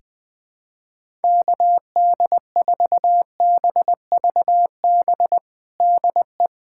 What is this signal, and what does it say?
Morse code, heard as a keyed tone, about 20 wpm, 700 Hz. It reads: KD4BVB DE
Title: USA Callsign DE Callsign - 6 Characters - 20wpm